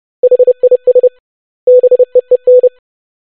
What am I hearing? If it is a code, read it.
Morse code, heard as a keyed tone, about 30 wpm, 500 Hz. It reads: HIS BEEN
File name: basic100-25wpm